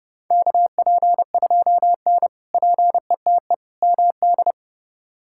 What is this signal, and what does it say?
Morse code, heard as a keyed tone, about 30 wpm, 700 Hz. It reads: KP2N PETE MD
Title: K1USN SST QSO Practice - 30wpm